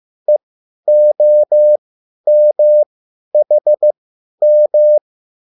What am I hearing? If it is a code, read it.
Morse code, heard as a keyed tone, about 15 wpm, spaced as for 10 wpm, 600 Hz. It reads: EOMHM